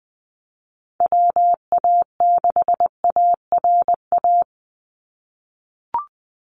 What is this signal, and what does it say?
Morse code, heard as a keyed tone, about 20 wpm, 700 Hz. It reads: WA6ARA
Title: NA SOTA Chaser Callsigns 20wpm